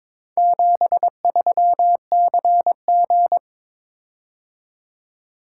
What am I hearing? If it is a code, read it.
Morse code, heard as a keyed tone, about 22 wpm, 700 Hz. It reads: 73CG
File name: practice / 22wpm / Four Letter-Number Combinations - 22wpm